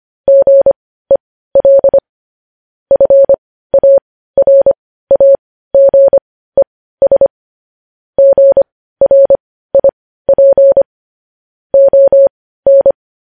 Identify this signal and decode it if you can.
Morse code, heard as a keyed tone, about 25 wpm, spaced as for 15 wpm, 550 Hz. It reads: GEL FARAGES GRIP ON
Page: news headlines in morse code at 15 wpm